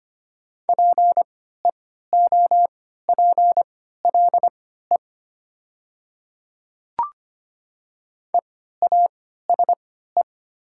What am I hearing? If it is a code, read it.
Morse code, heard as a keyed tone, about 25 wpm, spaced as for 14 wpm, 700 Hz. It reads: PEOPLE EASE
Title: 3x Character Spacing - Top 300 Words - 25wpm